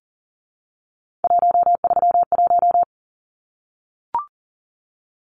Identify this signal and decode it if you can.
Morse code, heard as a keyed tone, about 40 wpm, 700 Hz. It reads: 131
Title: Numbers - 3-Digit 40wpm